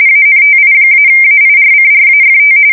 RTTY, about 45.45 baud. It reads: ANTI-AIRCRAFT
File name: fsk170 / battleshipnj